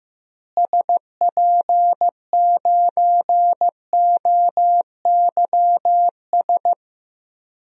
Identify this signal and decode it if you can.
Morse code, heard as a keyed tone, about 15 wpm, 700 Hz. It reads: SP9OYS
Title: DXCC-100 Call Signs - 6 Characters - Repeated 2x - 15wpm